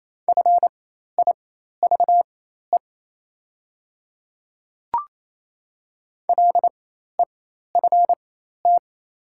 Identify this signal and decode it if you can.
Morse code, heard as a keyed tone, about 27 wpm, spaced as for 13 wpm, 700 Hz. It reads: FIVE LEFT
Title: 4x Character Spacing - Top 500 Words - 28wpm